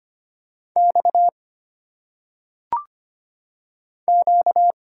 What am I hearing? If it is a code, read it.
Morse code, heard as a keyed tone, about 25 wpm, 700 Hz. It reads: X Q
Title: Single Letters - Phonetic Pronunciation 25wpm